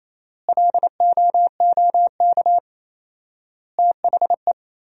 Morse code, heard as a keyed tone, about 28 wpm, 700 Hz. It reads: LOOK THE
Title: 4x Word-Spacing - Sentences from Top 100 Words - 28wpm